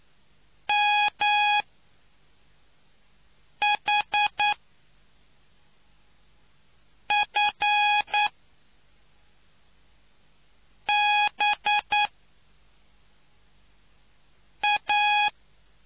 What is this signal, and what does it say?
Morse code, heard as a keyed tone, about 9 wpm, spaced as for 3 wpm, 830 Hz. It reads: MHFBA